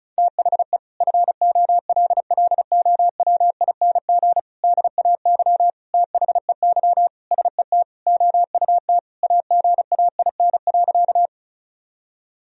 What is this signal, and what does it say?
Morse code, heard as a keyed tone, about 36 wpm, 700 Hz. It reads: THE FOLLOWING DAY THEY SET OUT AGAIN.
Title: Sentences from Top 1000 Words - Review Part 4 - 35wpm